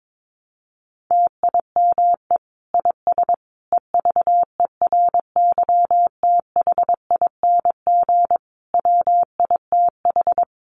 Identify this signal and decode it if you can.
Morse code, heard as a keyed tone, about 22 wpm, 700 Hz. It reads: TIME IS EVERYTHING WITH
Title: Sentences from Top 700 Words 22wpm